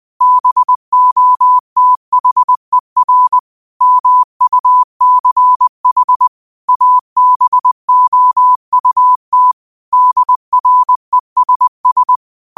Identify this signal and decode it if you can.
Morse code, heard as a keyed tone, about 20 wpm, 1000 Hz. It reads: BOTHER MUCH ABOUT DRESS